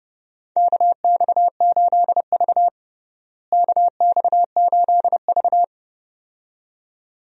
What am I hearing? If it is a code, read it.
Morse code, heard as a keyed tone, about 30 wpm, 700 Hz. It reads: KX8V KX8V
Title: USA Call Signs - 4 Characters - Repeated 2x - 30wpm